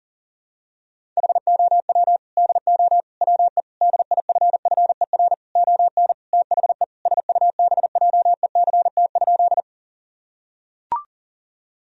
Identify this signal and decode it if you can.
Morse code, heard as a keyed tone, about 37 wpm, 700 Hz. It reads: HOW DO WE DIFFER ON THE SUBJECT?
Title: Sentences from Top 1000 Words - Review Part 1 - 40wpm